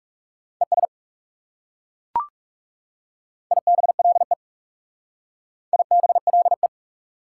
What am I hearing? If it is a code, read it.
Morse code, heard as a keyed tone, about 37 wpm, 690 Hz. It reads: ES IBLE IBLE
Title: Speed-Racing Prefixes and Suffixes 30wpm